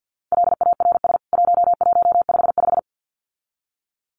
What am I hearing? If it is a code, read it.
Morse code, heard as a keyed tone, about 50 wpm, 700 Hz. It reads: LARS 1155
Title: CWT Exchange 50wpm